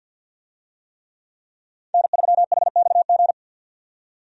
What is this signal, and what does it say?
Morse code, heard as a keyed tone, about 47 wpm, 700 Hz. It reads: N3HXG
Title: USA Call Signs - 5 Characters - 50wpm